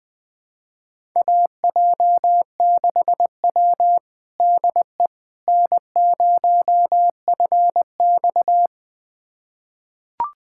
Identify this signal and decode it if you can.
Morse code, heard as a keyed tone, about 20 wpm, 700 Hz. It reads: AJ6W DE N0FX
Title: USA Callsign DE Callsign - 4 Characters - 20wpm